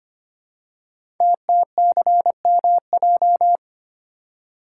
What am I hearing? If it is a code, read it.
Morse code, heard as a keyed tone, about 25 wpm, 700 Hz. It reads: TTCMJ
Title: Code Groups - 5 Letters - 25wpm